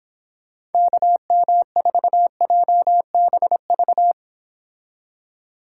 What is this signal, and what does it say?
Morse code, heard as a keyed tone, about 30 wpm, 700 Hz. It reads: KM4JBV